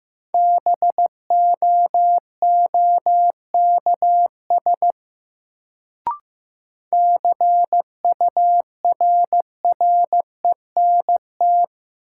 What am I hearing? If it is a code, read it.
Morse code, heard as a keyed tone, about 15 wpm, 700 Hz. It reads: BOOKS CURRENT
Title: Sentences from Top 1000 Words - Review Part 4 - New Word Drill - 15wpm